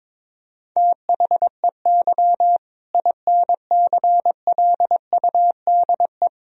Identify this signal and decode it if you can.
Morse code, heard as a keyed tone, about 21 wpm, 700 Hz. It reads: THEY INCLUDE
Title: Sentences from Top 1000 Words - Review Part 3 - 22wpm